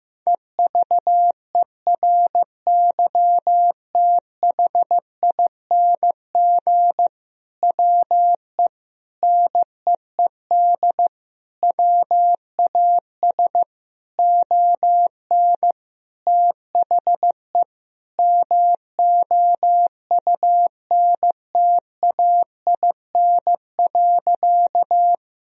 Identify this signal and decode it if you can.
Morse code, heard as a keyed tone, about 15 wpm, 700 Hz. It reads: EVERYTHING WE NEED WAS ON THE MOUNTAIN.
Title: Sentences from Top 1000 Words - Review Part 1 - 15wpm